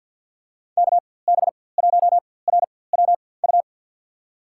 Morse code, heard as a keyed tone, about 49 wpm, spaced as for 23 wpm, 700 Hz. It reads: KB1RWU
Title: USA Call Signs - 6 Characters - 4x Character Spacing - 50wpm